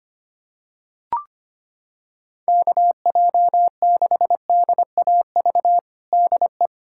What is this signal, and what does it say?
Morse code, heard as a keyed tone, about 25 wpm, 700 Hz. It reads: KJ6DAV DE
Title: USA Callsign DE Callsign - 6 Characters - 25wpm